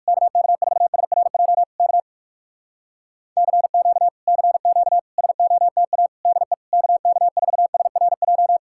Morse code, heard as a keyed tone, about 59 wpm, 700 Hz. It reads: KK4SRJ K CQ CQ SOTA DE KK4SRJ